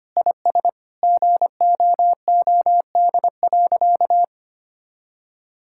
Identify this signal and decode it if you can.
Morse code, heard as a keyed tone, about 25 wpm, 700 Hz. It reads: IS GOOD.
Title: Sentences from Top 1000 Words - Review Part 6 - 25wpm